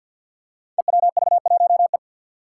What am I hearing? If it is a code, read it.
Morse code, heard as a keyed tone, about 50 wpm, 700 Hz. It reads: EWV1E
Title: Five Challenging Letter-Number Combinations 50wpm